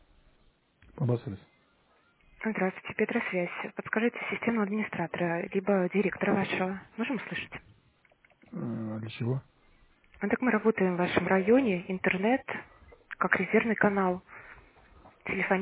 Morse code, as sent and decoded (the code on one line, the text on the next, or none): none